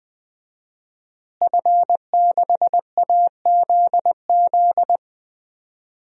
Morse code, heard as a keyed tone, about 20 wpm, 700 Hz. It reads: F6AZZ
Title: DXCC-100 Call Signs - 5 Characters - Repeated 3x - 20wpm